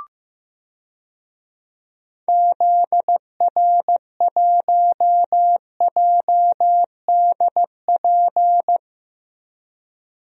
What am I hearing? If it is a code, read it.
Morse code, heard as a keyed tone, about 15 wpm, 700 Hz. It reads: ZR1JDP
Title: DXCC-100 Call Signs - 6 Characters - Repeated 2x - 15wpm